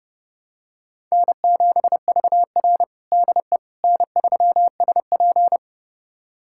Morse code, heard as a keyed tone, about 30 wpm, 700 Hz. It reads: N7VR DE N3SP
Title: USA Callsign DE Callsign - 4 Characters - 30wpm